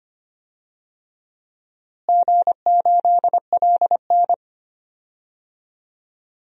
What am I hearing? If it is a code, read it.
Morse code, heard as a keyed tone, about 25 wpm, 700 Hz. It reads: G8LN
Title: DXCC-100 Call Signs - 4 Characters - Repeated 2x - 25wpm